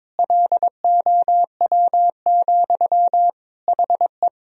Morse code, heard as a keyed tone, about 22 wpm, 700 Hz. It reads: LOW, HE
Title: Sentences from Top 1000 Words - Review Part 1 - 22wpm